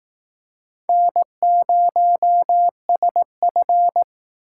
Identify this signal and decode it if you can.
Morse code, heard as a keyed tone, about 18 wpm, 700 Hz. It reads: N0SF
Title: Speed Racing - USA 1x2 Call Signs 15wpm